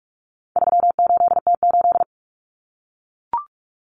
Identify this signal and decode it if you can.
Morse code, heard as a keyed tone, about 44 wpm, 700 Hz. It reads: 38T8
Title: Common Cut Numbers - 4-Digit 45wpm